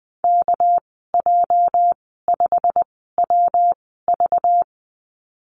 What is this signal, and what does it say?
Morse code, heard as a keyed tone, about 20 wpm, spaced as for 14 wpm, 700 Hz. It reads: KJ5WV
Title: 231 Lesson 37g - Letter x - Call Signs 20wpm